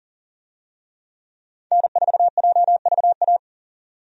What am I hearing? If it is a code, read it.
Morse code, heard as a keyed tone, about 40 wpm, 700 Hz. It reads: N4JVA
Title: USA Call Signs - 5 Characters - 40wpm